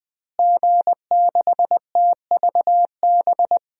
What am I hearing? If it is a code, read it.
Morse code, heard as a keyed tone, about 20 wpm, 700 Hz. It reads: G6TVB
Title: DXCC-100 Call Signs - 5 Characters - Repeated 3x - 20wpm